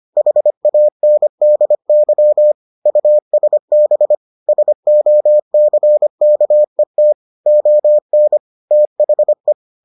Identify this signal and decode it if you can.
Morse code, heard as a keyed tone, about 25 wpm, 600 Hz. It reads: HANDY USB SOCKET ON THE